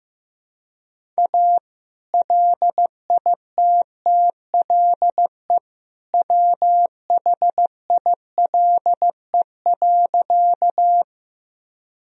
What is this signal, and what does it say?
Morse code, heard as a keyed tone, about 15 wpm, 700 Hz. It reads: A LITTLE WHILE.
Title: Sentences from Top 1000 Words - Review Part 1 - 15wpm